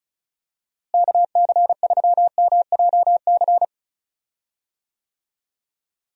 Morse code, heard as a keyed tone, about 34 wpm, 700 Hz. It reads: KC3MJC
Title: USA Call Signs - 6 Characters - Repeated 3x - 35wpm